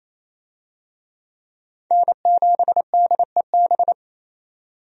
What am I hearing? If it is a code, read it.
Morse code, heard as a keyed tone, about 28 wpm, 700 Hz. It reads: N7DEB
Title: USA Call Signs - 5 Characters - 28wpm